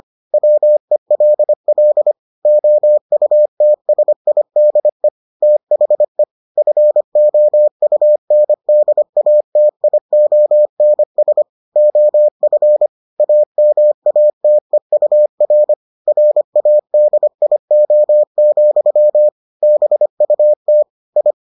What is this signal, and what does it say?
Morse code, heard as a keyed tone, about 25 wpm, 600 Hz. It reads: WELL OUTSIDE THE FOUNDATIONS OF AMATEUR RADIO, BUT I